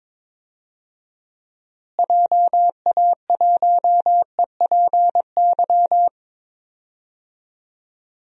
Morse code, heard as a keyed tone, about 22 wpm, 700 Hz. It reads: JA1EPY
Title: DXCC-100 Call Signs - 6 Characters - Repeated 2x - 22wpm